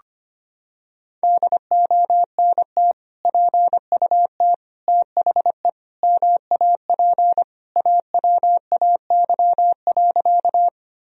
Morse code, heard as a keyed tone, about 25 wpm, 700 Hz. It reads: DONT PUT THE MAP AWAY.
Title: Sentences from Top 1000 Words - Review Part 2 - 25wpm